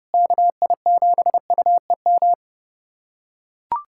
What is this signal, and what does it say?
Morse code, heard as a keyed tone, about 30 wpm, 700 Hz. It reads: KI7UEM
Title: USA Callsign DE Callsign - 6 Characters - 30wpm